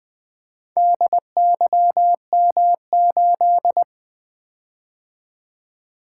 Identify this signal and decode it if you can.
Morse code, heard as a keyed tone, about 24 wpm, 700 Hz. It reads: DYM8
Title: Speed-Racing - Four Letter-Number Combinations - 20wpm